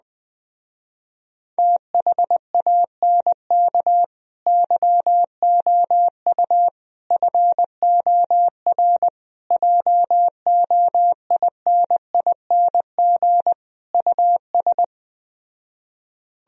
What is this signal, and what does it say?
Morse code, heard as a keyed tone, about 20 wpm, 700 Hz. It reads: THANK YOU FOR JOINING US
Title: Speed-Racing - Sets of 5 Words 17wpm